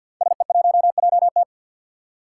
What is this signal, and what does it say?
Morse code, heard as a keyed tone, about 53 wpm, 700 Hz. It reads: SE1JT
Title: Five Challenging Letter-Number Combinations 50wpm